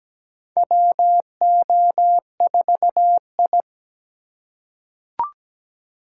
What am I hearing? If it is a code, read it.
Morse code, heard as a keyed tone, about 17 wpm, 700 Hz. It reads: WO4I
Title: USA Call Signs - 4 Characters - 3x Character Spacing - 17wpm